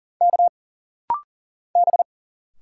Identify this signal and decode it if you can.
Morse code, heard as a keyed tone, about 40 wpm, 700 Hz. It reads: K B
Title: Single Letter-Number Alt - Rapid-Fire 40wpm